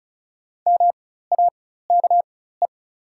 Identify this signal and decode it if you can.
Morse code, heard as a keyed tone, about 36 wpm, spaced as for 16 wpm, 700 Hz. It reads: MAKE
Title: 4x Character Spacing - Top 100 Words - 35wpm